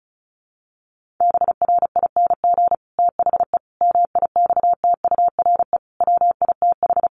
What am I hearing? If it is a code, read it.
Morse code, heard as a keyed tone, about 34 wpm, 700 Hz. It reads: BRING THE MIXTURE WITH
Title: Sentences from Top 1500 Words 35wpm